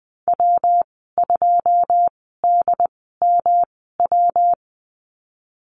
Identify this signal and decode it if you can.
Morse code, heard as a keyed tone, about 20 wpm, spaced as for 15 wpm, 700 Hz. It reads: W2DMW
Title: 200 Lesson 32g - Letter k - Call Signs 20wpm